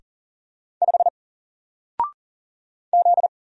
Morse code, heard as a keyed tone, about 39 wpm, 700 Hz. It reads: 5 Z
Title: Single Letter-Number 40wpm